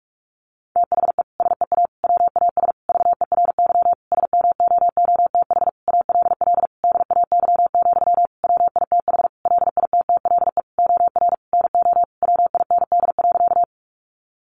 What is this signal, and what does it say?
Morse code, heard as a keyed tone, about 44 wpm, 700 Hz. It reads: THE SEA WAS VERY SMOOTH ALL DAY, WITH LITTLE OR NO WIND.
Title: Sentences from Top 1400 Words 45wpm